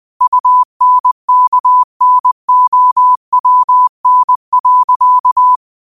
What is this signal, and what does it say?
Morse code, heard as a keyed tone, about 20 wpm, 1000 Hz. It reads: UNKNOWN.